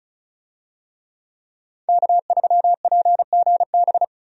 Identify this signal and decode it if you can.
Morse code, heard as a keyed tone, about 36 wpm, 700 Hz. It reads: K3PGB
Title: USA Call Signs - 5 Characters - 35wpm